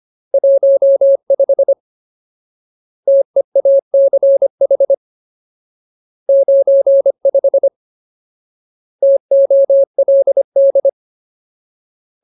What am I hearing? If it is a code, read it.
Morse code, heard as a keyed tone, about 25 wpm, 550 Hz. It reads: 15 TEACH 95 TOLD
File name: Ultimate Word List / 25WPM Ultimate